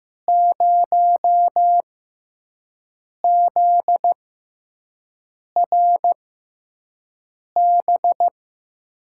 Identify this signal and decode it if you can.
Morse code, heard as a keyed tone, about 15 wpm, spaced as for 5 wpm, 700 Hz. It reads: 0ZRB
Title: USA Call Signs - 6 Characters - 6x Character Spacing - 15wpm